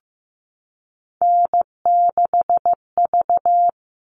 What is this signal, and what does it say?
Morse code, heard as a keyed tone, about 15 wpm, 700 Hz. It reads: N6V
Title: All Cut Numbers - 3-Digit 15wpm